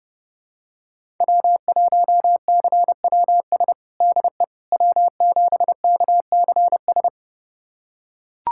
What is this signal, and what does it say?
Morse code, heard as a keyed tone, about 32 wpm, 700 Hz. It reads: W1CWS DE W7KCS
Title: USA Callsign DE Callsign - 5 Characters - 30wpm